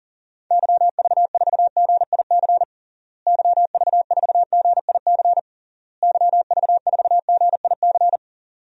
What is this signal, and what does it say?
Morse code, heard as a keyed tone, about 37 wpm, 700 Hz. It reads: YV4GIC YV4GIC YV4GIC
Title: DXCC-100 Call Signs - 6 Characters - Repeated 3x - 40wpm